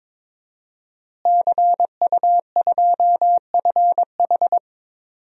Morse code, heard as a keyed tone, about 22 wpm, 700 Hz. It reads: CU2FH
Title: DXCC-100 Call Signs - 5 Characters - Repeated 3x - 22wpm